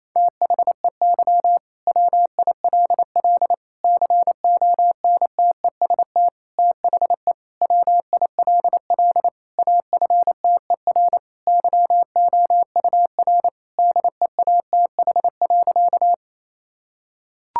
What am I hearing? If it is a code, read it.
Morse code, heard as a keyed tone, about 28 wpm, 700 Hz. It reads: THEY WILL CONTEST THE WILL AFTER YOUR DEATH.
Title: Sentences from Top 1600 Words 28wpm